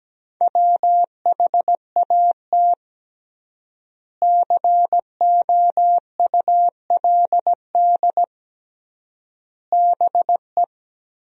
Morse code, heard as a keyed tone, about 17 wpm, 700 Hz. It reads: WHAT COULD BE
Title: 3x Word-Spacing - Sentences from Top 400 Words - 17wpm